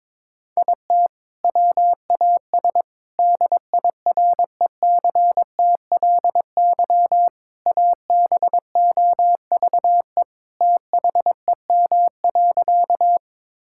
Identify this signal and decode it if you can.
Morse code, heard as a keyed tone, about 22 wpm, 700 Hz. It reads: IT WAS DIRECTLY ABOVE THEM.
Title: Sentences from Top 1000 Words - Review Part 5 - 22wpm